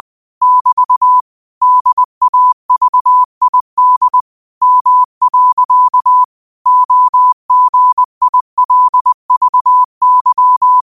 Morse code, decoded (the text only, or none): <BT> DAVID M. OGILVY